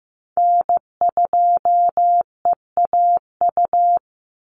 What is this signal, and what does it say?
Morse code, heard as a keyed tone, about 15 wpm, 700 Hz. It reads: N2EAU
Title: USA Call Signs 15wpm